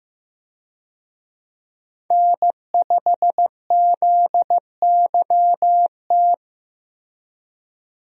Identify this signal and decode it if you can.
Morse code, heard as a keyed tone, about 15 wpm, 700 Hz. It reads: N5ZYT
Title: USA Call Signs - 5 Characters - Repeated 2x - 15wpm